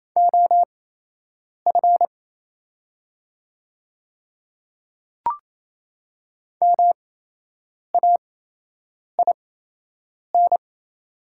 Morse code, heard as a keyed tone, about 28 wpm, spaced as for 8 wpm, 690 Hz. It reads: OF MAIN
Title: 8x Character Spacing - Top 300 Words - 28wpm